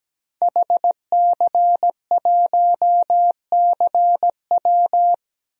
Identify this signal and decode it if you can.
Morse code, heard as a keyed tone, about 17 wpm, 700 Hz. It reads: HC1CW
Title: DXCC-100 Call Signs - 5 Characters - Repeated 3x - 17wpm